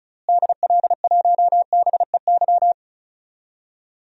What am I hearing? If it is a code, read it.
Morse code, heard as a keyed tone, about 34 wpm, 700 Hz. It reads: DL1BEY
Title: DXCC-100 Call Signs - 6 Characters - Repeated 2x - 35wpm